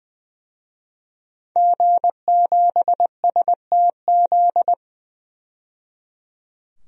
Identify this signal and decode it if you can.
Morse code, heard as a keyed tone, about 20 wpm, 700 Hz. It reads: G7STZ
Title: DXCC-100 Call Signs - 5 Characters - 20wpm